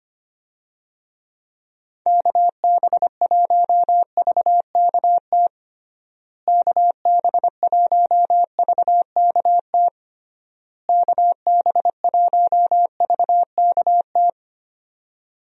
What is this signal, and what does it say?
Morse code, heard as a keyed tone, about 25 wpm, 700 Hz. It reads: KB1VKT KB1VKT KB1VKT
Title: USA Call Signs - 6 Characters - Repeated 3x - 25wpm